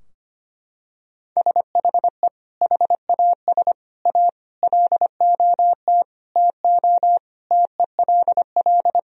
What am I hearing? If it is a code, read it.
Morse code, heard as a keyed tone, about 25 wpm, 700 Hz. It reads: SHE HAS A LOT TO TELL
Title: Sentences from Top 1000 Words - Review Part 3 - 25wpm